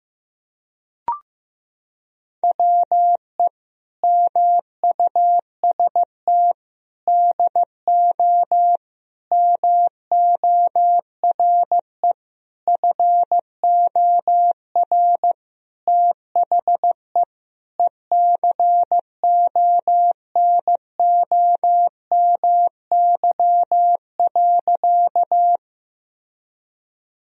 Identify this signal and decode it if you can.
Morse code, heard as a keyed tone, about 15 wpm, 700 Hz. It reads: WE MUST DO MORE FOR THE ECONOMY.
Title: Sentences from Top 1000 Words - Review Part 3 - 15wpm